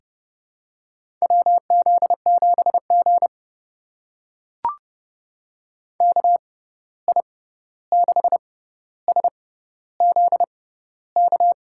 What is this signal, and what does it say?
Morse code, heard as a keyed tone, about 30 wpm, spaced as for 14 wpm, 700 Hz. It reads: WZ7G KI6SZK
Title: USA Call Signs - 6x Character Spacing - 30wpm